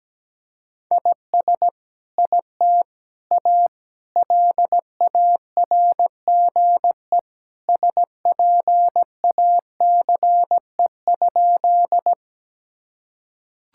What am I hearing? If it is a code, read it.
Morse code, heard as a keyed tone, about 17 wpm, 700 Hz. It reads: IS IT A LARGE SPACE?
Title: Sentences from Top 1000 Words - Review Part 5 - 17wpm